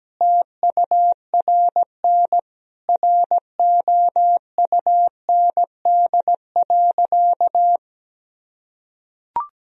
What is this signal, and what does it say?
Morse code, heard as a keyed tone, about 17 wpm, 700 Hz. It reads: TURN ROUND.
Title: Sentences from Top 1000 Words - Review Part 1 - 17wpm